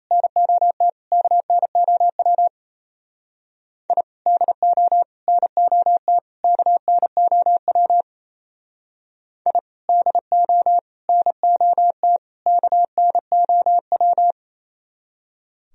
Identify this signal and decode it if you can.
Morse code, heard as a keyed tone, about 34 wpm, 700 Hz. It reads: NOT KNOW I DO NOT KNOW I DO NOT KNOW
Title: Speed-Racing - Sets of 4 Words 28wpm